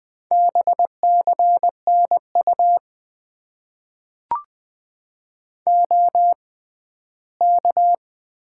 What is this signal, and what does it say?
Morse code, heard as a keyed tone, about 20 wpm, spaced as for 12 wpm, 700 Hz. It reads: BCN U OK
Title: Top 90 QSO Elements - 6x Character Spacing - 20wpm